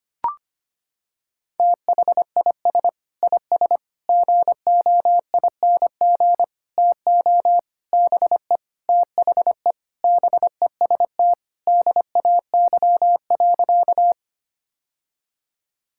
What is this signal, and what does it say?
Morse code, heard as a keyed tone, about 25 wpm, 700 Hz. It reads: THIS IS GOING TO BE THE BEST DAY.
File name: practice / 25wpm / Sentences with 4-Word Phrases - 25wpm